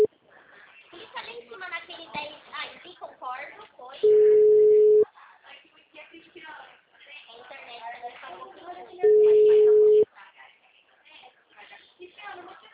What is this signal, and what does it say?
Ringback tone (Europe, 425 Hz), with speech.
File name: out- -2012- - - .44